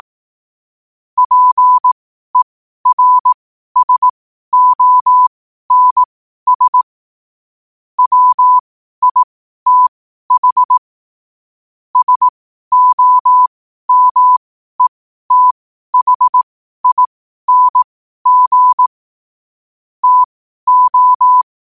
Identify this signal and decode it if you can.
Morse code, heard as a keyed tone, about 18 wpm, spaced as for 12 wpm, 1000 Hz. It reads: PERSONS WITH SOMETHING TO
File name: qotdfiles / 12wpm